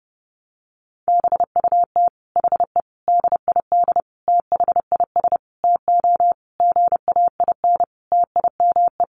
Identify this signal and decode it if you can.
Morse code, heard as a keyed tone, about 30 wpm, 700 Hz. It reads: BUT HE DID THIS TO GAIN TIME
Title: Sentences from Top 1400 Words 30wpm